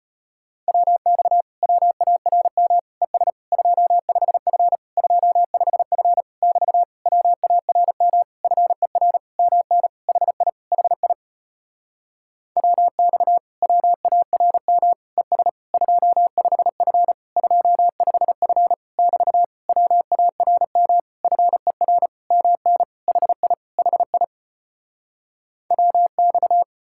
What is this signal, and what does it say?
Morse code, heard as a keyed tone, about 37 wpm, 700 Hz. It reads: WX WARM ES 25F 25F <BT> WARM FER MN HI HI WX WARM ES 25F 25F <BT> WARM FER MN HI HI WX